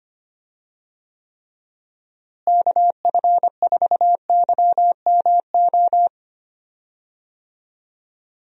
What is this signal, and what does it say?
Morse code, heard as a keyed tone, about 25 wpm, 700 Hz. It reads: KF4YMO
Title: USA Call Signs - 6 Characters - Repeated 3x - 25wpm